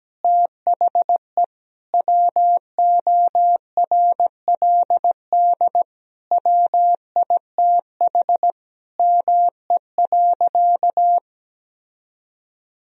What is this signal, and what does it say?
Morse code, heard as a keyed tone, about 17 wpm, 700 Hz. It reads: THE WORLD WITH ME.
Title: Sentences from Top 1000 Words - Review Part 3 - 17wpm